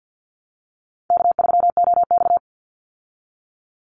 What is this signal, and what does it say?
Morse code, heard as a keyed tone, about 50 wpm, 700 Hz. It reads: K3OX
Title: Easy Call Signs Repeated 2x - 50wpm